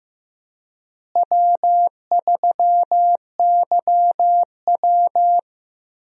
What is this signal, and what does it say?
Morse code, heard as a keyed tone, about 15 wpm, 700 Hz. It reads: W3YW
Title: USA Call Signs - 4 Characters - Repeated 3x - 15wpm